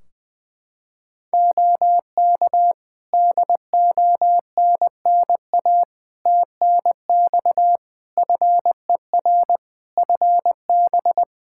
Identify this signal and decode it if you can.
Morse code, heard as a keyed tone, about 21 wpm, 700 Hz. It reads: OK DONNA TNX FER FB
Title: Practice QSO - Exchange 4 - Ending - 20wpm